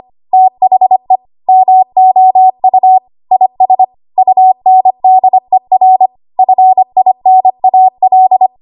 Morse code, heard as a keyed tone, about 25 wpm, 750 Hz. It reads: THE MOU IS UNDER FINAL
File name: Archive / 25 WPM / 25WPM